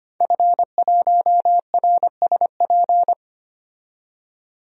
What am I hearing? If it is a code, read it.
Morse code, heard as a keyed tone, about 25 wpm, 700 Hz. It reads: F1RSP
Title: DXCC-100 Call Signs - 5 Characters - 25wpm